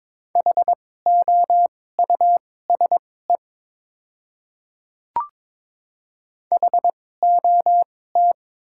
Morse code, heard as a keyed tone, about 21 wpm, spaced as for 16 wpm, 700 Hz. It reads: HOUSE HOT
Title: 2x Character Spacing - Top 200 Words - 22wpm